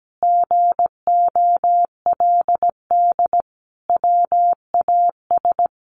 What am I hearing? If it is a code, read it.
Morse code, heard as a keyed tone, about 17 wpm, 700 Hz. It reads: GOLD WAS
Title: Sentences from Top 1100 Words 17wpm